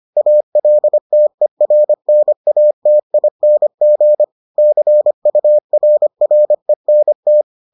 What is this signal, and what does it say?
Morse code, heard as a keyed tone, about 25 wpm, 600 Hz. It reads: ALTERNATING CURRENT